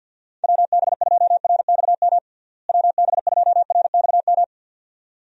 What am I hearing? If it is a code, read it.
Morse code, heard as a keyed tone, about 50 wpm, 700 Hz. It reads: WB2RXM WB2RXM
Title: USA Call Signs - 6 Characters - Repeated 2x - 50wpm